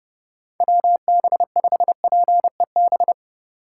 Morse code, heard as a keyed tone, about 30 wpm, 700 Hz. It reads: WB5PEB
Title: USA Call Signs - 6 Characters - 4x Character Spacing - 30wpm